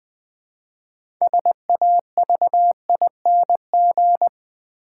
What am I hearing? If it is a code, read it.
Morse code, heard as a keyed tone, about 20 wpm, 700 Hz. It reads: SAVING
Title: Build A Word - 3 Letters At A Time - 6-Letter Words - 20wpm